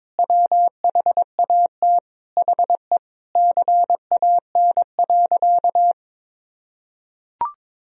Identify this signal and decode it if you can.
Morse code, heard as a keyed tone, about 21 wpm, 700 Hz. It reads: WHAT HE CAN.
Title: Sentences from Top 1000 Words - Review Part 1 - 22wpm